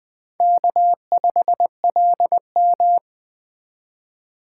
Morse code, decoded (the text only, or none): K5LM